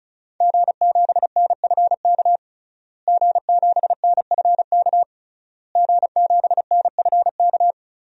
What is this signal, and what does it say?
Morse code, heard as a keyed tone, about 36 wpm, 700 Hz. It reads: G7NFK G7NFK G7NFK
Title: DXCC-100 Call Signs - 5 Characters - Repeated 3x - 35wpm